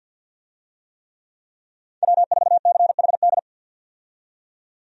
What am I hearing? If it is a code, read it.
Morse code, heard as a keyed tone, about 52 wpm, 700 Hz. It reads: W4CHD
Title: USA Call Signs - 5 Characters - 50wpm